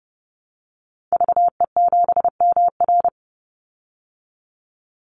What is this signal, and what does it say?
Morse code, heard as a keyed tone, about 30 wpm, 700 Hz. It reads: VE7MR
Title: Call Signs Repeated 3x - 30wpm